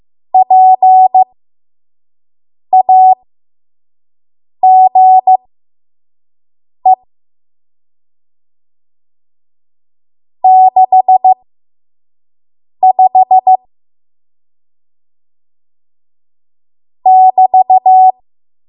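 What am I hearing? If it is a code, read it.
Morse code, heard as a keyed tone, about 15 wpm, spaced as for 5 wpm, 750 Hz. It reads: PAGE 65 <BT>
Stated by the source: 5 WPM Morse code practice.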